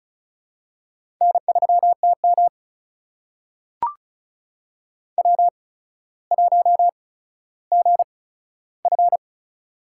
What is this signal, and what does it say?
Morse code, heard as a keyed tone, about 36 wpm, spaced as for 15 wpm, 700 Hz. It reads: N3TM W1GF
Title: USA Call Signs - 4 Characters - 8x Character Spacing - 35wpm